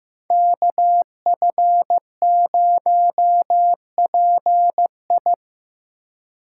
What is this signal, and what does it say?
Morse code, heard as a keyed tone, about 15 wpm, 700 Hz. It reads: KF0PI
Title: USA Call Signs - 5 Characters - 2x Character Spacing - 15wpm